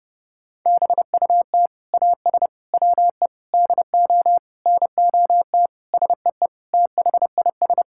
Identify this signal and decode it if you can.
Morse code, heard as a keyed tone, about 30 wpm, 700 Hz. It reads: BUT AS WE DO NOT SEE THIS
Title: Sentences from Top 1600 Words 30wpm